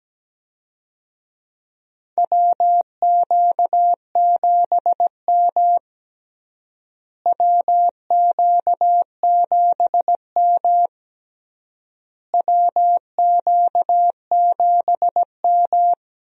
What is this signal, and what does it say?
Morse code, heard as a keyed tone, about 17 wpm, 700 Hz. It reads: WQ7M WQ7M WQ7M
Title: USA Call Signs - 4 Characters - Repeated 3x - 17wpm